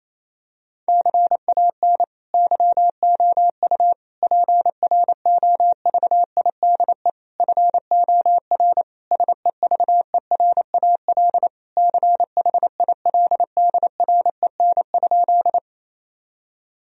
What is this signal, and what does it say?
Morse code, heard as a keyed tone, about 28 wpm, 700 Hz. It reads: CAN YOU PROVIDE FOR SEVERAL CHILDREN?
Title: Sentences from Top 1000 Words - Review Part 2 - 28wpm